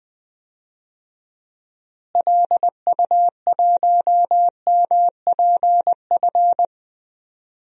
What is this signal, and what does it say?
Morse code, heard as a keyed tone, about 21 wpm, 700 Hz. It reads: LU1MPF
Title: DXCC-100 Call Signs - 6 Characters - Repeated 2x - 20wpm